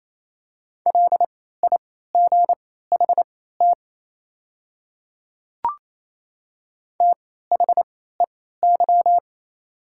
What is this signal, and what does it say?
Morse code, heard as a keyed tone, about 28 wpm, spaced as for 16 wpm, 690 Hz. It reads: LIGHT THEY
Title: 3x Character Spacing - Top 200 Words - 28wpm